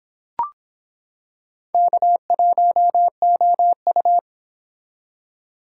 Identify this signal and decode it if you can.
Morse code, heard as a keyed tone, about 25 wpm, 700 Hz. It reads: K1OU